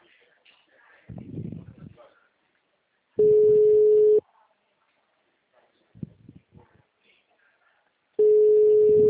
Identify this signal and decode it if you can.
Ringback tone (Europe, 425 Hz), with speech.